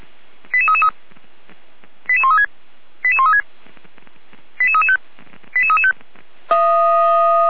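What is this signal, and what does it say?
Five-tone selective call (ZVEI), heard as quick runs of five tones, tones of about 70 ms. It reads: 88333 88136 88136 88335 88335
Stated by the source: ZVEI five-tone selective call 88402, received on an SDR, from the feuerwehr-orlamuende.org fire brigade audio archive.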